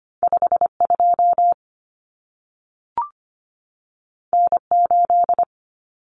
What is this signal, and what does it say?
Morse code, heard as a keyed tone, about 25 wpm, 700 Hz. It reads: 52 N8
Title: Common Cut Numbers - 2-Digit 25wpm